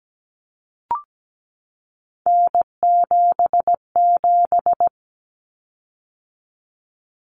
Morse code, heard as a keyed tone, about 17 wpm, 700 Hz. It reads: N77
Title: Common Cut Numbers - 3-Digit 17wpm